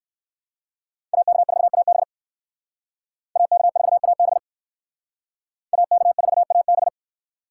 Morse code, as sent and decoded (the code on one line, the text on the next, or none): .- -.- ....- .- -... / .- -.- ....- .- -... / .- -.- ....- .- -...
AK4AB AK4AB AK4AB